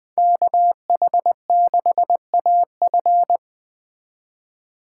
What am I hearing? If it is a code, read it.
Morse code, heard as a keyed tone, about 20 wpm, 700 Hz. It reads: KH6AF
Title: Speed Racing - USA 2x2 Call Signs 17wpm